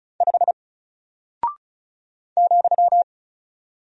Morse code, heard as a keyed tone, about 36 wpm, 700 Hz. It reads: 5 ,
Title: Single Character (Letter:Number:Punctuation:Prosign) Alt 35wpm